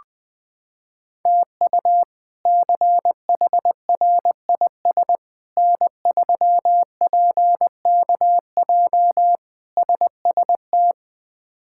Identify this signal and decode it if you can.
Morse code, heard as a keyed tone, about 20 wpm, 700 Hz. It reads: TU CHRIS N3PKJ SST
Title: K1USN SST QSO Practice - 20wpm